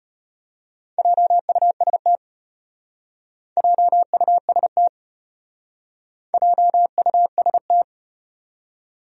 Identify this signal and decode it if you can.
Morse code, heard as a keyed tone, about 33 wpm, 700 Hz. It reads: JUST JUST JUST